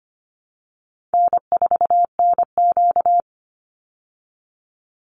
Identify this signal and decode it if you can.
Morse code, heard as a keyed tone, about 25 wpm, 700 Hz. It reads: N4NQ
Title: Easy Call Signs Repeated 2x - 25wpm